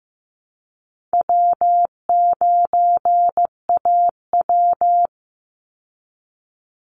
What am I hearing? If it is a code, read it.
Morse code, heard as a keyed tone, about 15 wpm, 700 Hz. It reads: W9AW
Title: Call Signs Repeated 3x - 15wpm